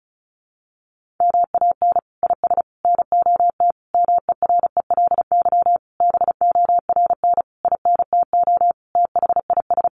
Morse code, heard as a keyed tone, about 34 wpm, 700 Hz. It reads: MAN IS NOT MERELY BORN INTO THIS
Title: Sentences from Top 1400 Words 35wpm